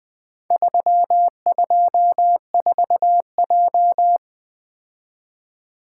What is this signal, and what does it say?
Morse code, heard as a keyed tone, about 20 wpm, 700 Hz. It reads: 324J
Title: Four Letter-Number Combinations - 20wpm